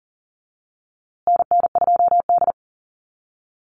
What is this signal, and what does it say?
Morse code, heard as a keyed tone, about 40 wpm, 700 Hz. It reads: NN2D
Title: USA Call Signs Repeated 2x 40wpm